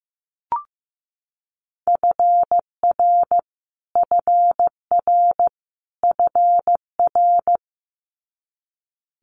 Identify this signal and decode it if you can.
Morse code, heard as a keyed tone, about 15 wpm, 700 Hz. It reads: FR FR FR
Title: Multi-letter Phonemes Repeated 3x 15wpm